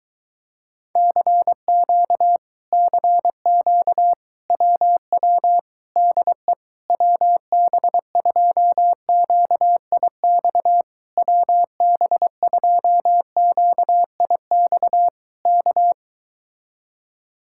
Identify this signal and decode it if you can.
Morse code, heard as a keyed tone, about 22 wpm, 700 Hz. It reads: CQ CQ WW DE WB2QIX WB2QIX K